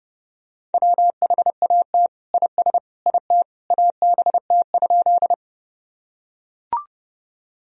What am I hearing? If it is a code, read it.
Morse code, heard as a keyed tone, about 32 wpm, 700 Hz. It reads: WHAT IS IT ABT?
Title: Sentences from Top 100 Words and Abbreviations 30wpm